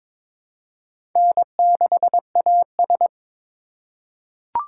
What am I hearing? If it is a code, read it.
Morse code, heard as a keyed tone, about 23 wpm, 700 Hz. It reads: N6AS
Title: USA Call Signs - 4 Characters - 2x Character Spacing - 22wpm